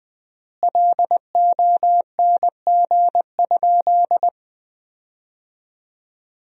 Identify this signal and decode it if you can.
Morse code, heard as a keyed tone, about 20 wpm, 700 Hz. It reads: LONG?
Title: 3x Word-Spacing - Sentences from Top 200 Words - 20wpm